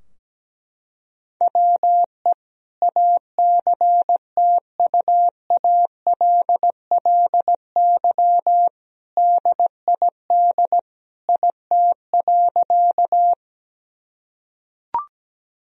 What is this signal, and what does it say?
Morse code, heard as a keyed tone, about 17 wpm, 700 Hz. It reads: WE ACTUALLY DID IT.
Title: Sentences from Top 1000 Words - Review Part 4 - 17wpm